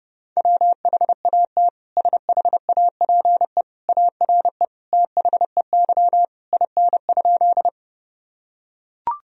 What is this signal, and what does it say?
Morse code, heard as a keyed tone, about 30 wpm, 700 Hz. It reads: WHAT SHAPE ARE THEY IN?
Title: Sentences from Top 1000 Words - Review Part 3 - 30wpm